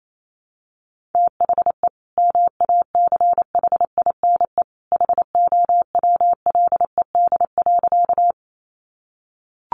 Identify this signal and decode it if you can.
Morse code, heard as a keyed tone, about 28 wpm, 700 Hz. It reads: THE MACHINE HOWLED.
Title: A Sound Of Thunder 28wpm